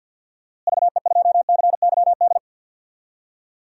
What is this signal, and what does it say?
Morse code, heard as a keyed tone, about 49 wpm, 700 Hz. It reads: VE2CYD
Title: DXCC-100 Call Signs - 6 Characters - Repeated 2x - 50wpm